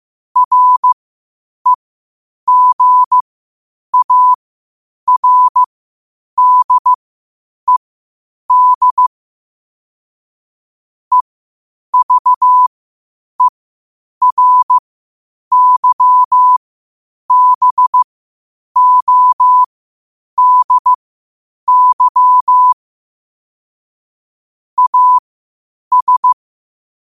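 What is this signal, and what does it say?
Morse code, heard as a keyed tone, about 15 wpm, spaced as for 8 wpm, 1000 Hz. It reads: REGARDED EVERYBODY AS
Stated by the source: Morse code at 8 wpm.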